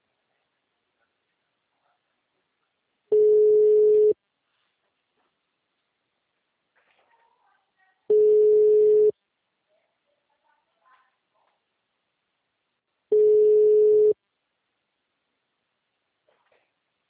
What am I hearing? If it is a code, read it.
Ringback tone (Europe, 425 Hz).
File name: out- -2026- - - .163